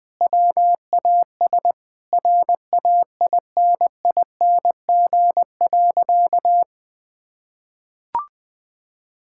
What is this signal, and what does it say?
Morse code, heard as a keyed tone, about 21 wpm, 700 Hz. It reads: WAS RAINING.
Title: Sentences with 3-Word Phrases - Spoken Beginning - 20wpm